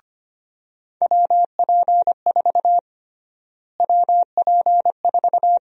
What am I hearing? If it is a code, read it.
Morse code, heard as a keyed tone, about 25 wpm, 700 Hz. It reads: WP4 WP4
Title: USA Call Sign Prefixes - Repeated 2x - 25wpm